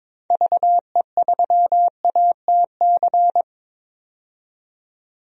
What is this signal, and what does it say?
Morse code, heard as a keyed tone, about 22 wpm, 700 Hz. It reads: VE3ATC
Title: DXCC-100 Call Signs - 6 Characters - 22wpm